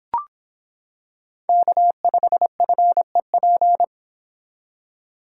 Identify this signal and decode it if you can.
Morse code, heard as a keyed tone, about 26 wpm, 700 Hz. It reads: K5FEP